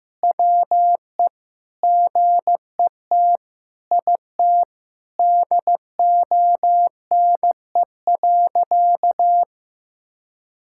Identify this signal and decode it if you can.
Morse code, heard as a keyed tone, about 15 wpm, 700 Hz. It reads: WE GET IT DONE.
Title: Sentences from Top 1600 Words 15wpm